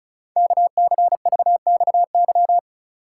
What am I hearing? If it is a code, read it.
Morse code, heard as a keyed tone, about 36 wpm, 700 Hz. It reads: KCVXY
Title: Code Groups - 5 Letters - 35wpm